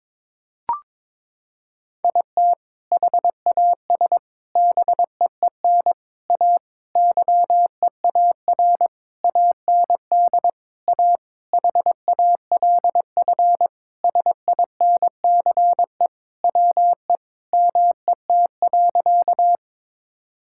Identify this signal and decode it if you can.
Morse code, heard as a keyed tone, about 22 wpm, 700 Hz. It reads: IT HAS BEEN A YEAR AND A HALF SINCE WE MET.
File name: practice / 22wpm / Sentences with 4-Word Phrases - 22wpm